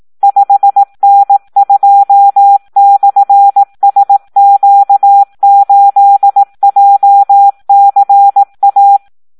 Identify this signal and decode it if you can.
Morse code, heard as a keyed tone, about 18 wpm, 800 Hz. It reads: 5N2/SQ8JCA
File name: Audio practica CW / SD 1 18